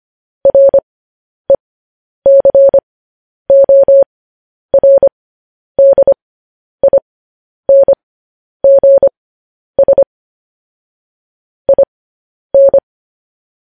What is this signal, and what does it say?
Morse code, heard as a keyed tone, about 25 wpm, spaced as for 10 wpm, 550 Hz. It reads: RECORDINGS IN
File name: podcast / News headlines-1x-25x10